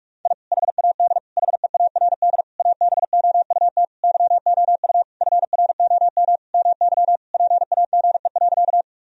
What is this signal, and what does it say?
Morse code, heard as a keyed tone, about 46 wpm, 700 Hz. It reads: I HAD HEARD ABOUT YOU FROM MY PAGE.
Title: Sentences from Top 1000 Words - Review Part 1 - 45wpm